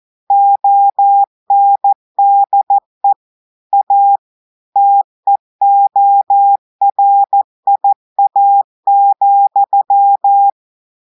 Morse code, heard as a keyed tone, about 14 wpm, 800 Hz. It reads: ONDE A TEORIA,
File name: audios / portugais / portugais texte en clair 1 vitesse 14 mots minutes